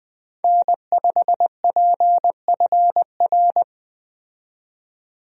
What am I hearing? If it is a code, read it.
Morse code, heard as a keyed tone, about 20 wpm, 700 Hz. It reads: N5PFR
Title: USA Call Signs - 5 Characters - 20wpm